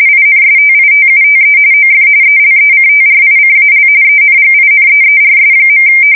RTTY, about 45.45 baud. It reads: INSTITUTE OF ELECTRICAL ENGINEERS.